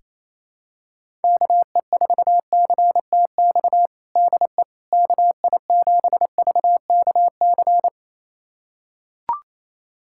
Morse code, heard as a keyed tone, about 28 wpm, 700 Hz. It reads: KE4CTX DE KI7VKC
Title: USA Callsign DE Callsign - 6 Characters - 28wpm